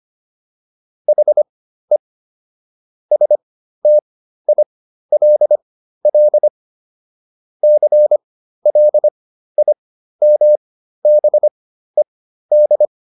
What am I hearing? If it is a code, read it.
Morse code, heard as a keyed tone, about 25 wpm, spaced as for 13 wpm, 600 Hz. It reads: HE STILL CLIMBED